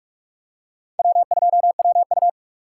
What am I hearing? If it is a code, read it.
Morse code, heard as a keyed tone, about 46 wpm, 700 Hz. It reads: W2WU
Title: USA Call Signs - 4 Characters - 3x Character Spacing - 45wpm